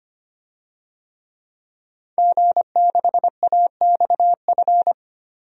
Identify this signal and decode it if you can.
Morse code, heard as a keyed tone, about 25 wpm, 700 Hz. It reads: G6AXF
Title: DXCC-100 Call Signs - 5 Characters - 25wpm